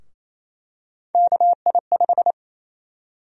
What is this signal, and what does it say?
Morse code, heard as a keyed tone, about 28 wpm, 700 Hz. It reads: KI5
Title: USA Call Sign Prefixes - 28wpm